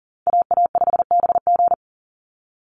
Morse code, heard as a keyed tone, about 40 wpm, 700 Hz. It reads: AA5BG
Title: Call Signs - 40wpm